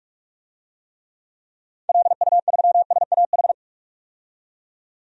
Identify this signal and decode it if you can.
Morse code, heard as a keyed tone, about 46 wpm, 700 Hz. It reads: LU3SAH
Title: DXCC-100 Call Signs - 6 Characters - 45wpm